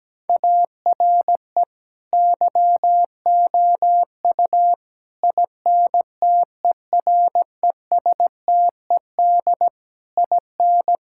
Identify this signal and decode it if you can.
Morse code, heard as a keyed tone, about 17 wpm, 700 Hz. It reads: ARE YOU INTERESTED IN
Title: Sentences from Top 1000 Words - Review Part 6 - 17wpm